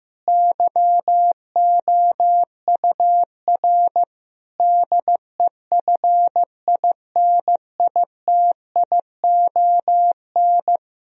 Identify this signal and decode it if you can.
Morse code, heard as a keyed tone, about 15 wpm, 700 Hz. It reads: YOUR DEFINITION
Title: Sentences from Top 1000 Words - Review Part 4 - 15wpm